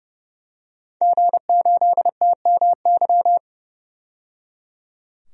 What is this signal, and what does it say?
Morse code, heard as a keyed tone, about 30 wpm, 700 Hz. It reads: G8TMY
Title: DXCC-100 Call Signs - 5 Characters - 30wpm